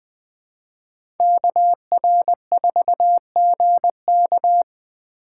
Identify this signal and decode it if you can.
Morse code, heard as a keyed tone, about 20 wpm, 700 Hz. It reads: KR4GK
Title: USA Call Signs - 2x Character Spacing - 20wpm